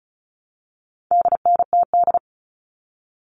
Morse code, heard as a keyed tone, about 34 wpm, 700 Hz. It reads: DNTD
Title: All Cut Numbers - 4-Digit 35wpm